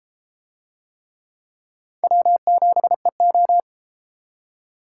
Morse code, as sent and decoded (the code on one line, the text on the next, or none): .-- --... ---
W7EO